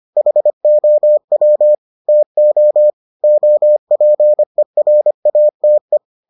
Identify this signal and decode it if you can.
Morse code, heard as a keyed tone, about 25 wpm, 600 Hz. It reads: HOW TO OPERATE